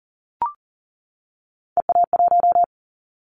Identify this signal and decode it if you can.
Morse code, heard as a keyed tone, about 40 wpm, 700 Hz. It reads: EA1
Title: Callsign Prefixes for Top 25 DXCC - 40wpm